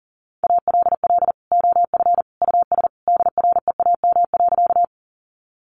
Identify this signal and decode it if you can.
Morse code, heard as a keyed tone, about 40 wpm, 700 Hz. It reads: ALL OF US DREAM.
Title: Sentences from Top 1500 Words 40wpm